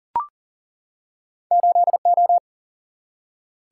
Morse code, heard as a keyed tone, about 40 wpm, 700 Hz. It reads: 8 O
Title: Single-Letter Single-Number 40wpm